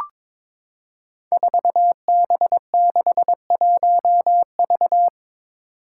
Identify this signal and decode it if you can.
Morse code, heard as a keyed tone, about 21 wpm, 700 Hz. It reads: 4B61V
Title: Five Challenging Letter-Number Combinations 22wpm